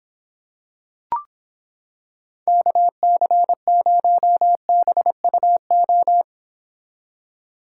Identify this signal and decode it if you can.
Morse code, heard as a keyed tone, about 26 wpm, 700 Hz. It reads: KC0BUO